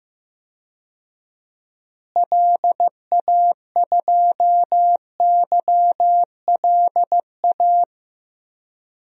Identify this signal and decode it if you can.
Morse code, heard as a keyed tone, about 14 wpm, 700 Hz. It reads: LA2YLA
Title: DXCC-100 Call Signs - 6 Characters - Repeated 3x - 15wpm